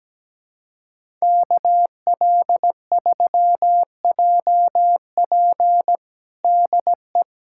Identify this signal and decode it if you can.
Morse code, heard as a keyed tone, about 17 wpm, 700 Hz. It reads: KL3JP DE
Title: USA Callsign DE Callsign - 5 Characters - 17wpm